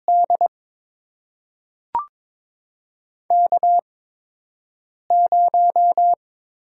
Morse code, heard as a keyed tone, about 22 wpm, spaced as for 6 wpm, 700 Hz. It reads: D K0
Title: USA Call Signs - 4 Characters - 8x Character Spacing - 22wpm